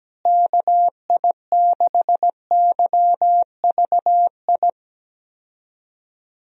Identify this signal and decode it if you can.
Morse code, heard as a keyed tone, about 17 wpm, 700 Hz. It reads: KI6YVI
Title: USA Call Signs - 6 Characters - Repeated 2x - 17wpm